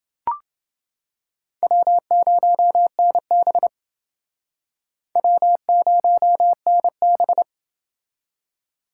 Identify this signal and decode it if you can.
Morse code, heard as a keyed tone, about 29 wpm, 700 Hz. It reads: W0NB W0NB